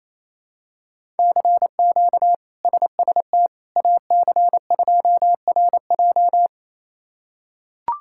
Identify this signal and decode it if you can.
Morse code, heard as a keyed tone, about 28 wpm, 700 Hz. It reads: CQ SST AC2RJ
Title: K1USN SST QSO Practice - 28wpm